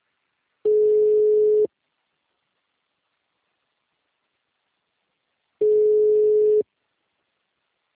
Ringback tone (Europe, 425 Hz).